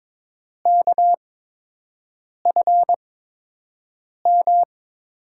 Morse code, heard as a keyed tone, about 22 wpm, spaced as for 6 wpm, 700 Hz. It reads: KFM
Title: USA Call Signs - 6 Characters - 8x Character Spacing - 22wpm